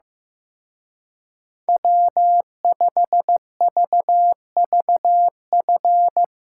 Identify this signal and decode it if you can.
Morse code, heard as a keyed tone, about 16 wpm, 700 Hz. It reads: W5VVF
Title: Speed Racing - USA 1x3 Call Signs 15wpm